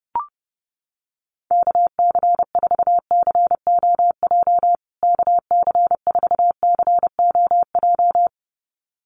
Morse code, heard as a keyed tone, about 30 wpm, 700 Hz. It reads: KC4COJ KC4COJ
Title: USA Call Signs Repeated 2x 30wpm